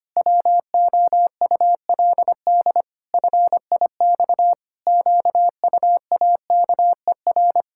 Morse code, heard as a keyed tone, about 25 wpm, 700 Hz. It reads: WOULD FIX QUAKER
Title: Pangram Challenge v2 - 25wpm